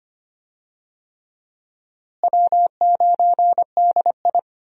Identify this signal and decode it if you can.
Morse code, heard as a keyed tone, about 25 wpm, 700 Hz. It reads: W9DI
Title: USA Call Signs - 4 Characters - Repeated 3x - 25wpm